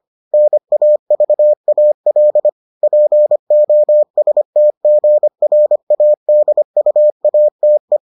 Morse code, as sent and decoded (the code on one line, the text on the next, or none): -. .- ...- .- .-.. / .--. --- ... - --. .-. .- -.. ..- .- - .
NAVAL POSTGRADUATE